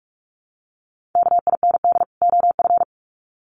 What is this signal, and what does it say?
Morse code, heard as a keyed tone, about 44 wpm, 700 Hz. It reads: KIND OF
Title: Sets of 2 Words 45wpm